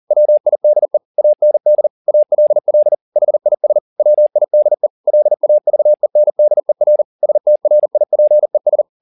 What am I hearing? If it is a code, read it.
Morse code, heard as a keyed tone, about 40 wpm, 600 Hz. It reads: WIDE AND ALL HIS WIDE LAVENDER STRIPES